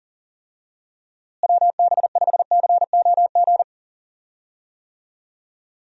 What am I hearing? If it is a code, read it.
Morse code, heard as a keyed tone, about 40 wpm, 700 Hz. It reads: WB5COG
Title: USA Call Signs - 6 Characters - Repeated 3x - 40wpm